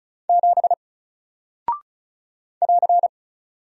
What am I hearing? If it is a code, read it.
Morse code, heard as a keyed tone, about 36 wpm, 700 Hz. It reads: 7 <AR>
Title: Single Character (Letter:Number:Punctuation:Prosign) Alt 35wpm